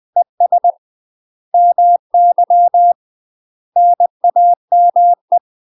Morse code, heard as a keyed tone, about 20 wpm, 700 Hz. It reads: ES MY NAME
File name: UK MASTER QSOs TRAINING / 20WPM / UK MASTER QSOs 0021